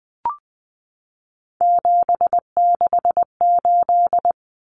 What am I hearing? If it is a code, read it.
Morse code, heard as a keyed tone, about 20 wpm, 700 Hz. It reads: 768
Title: Common Cut Numbers - 3-Digit 20wpm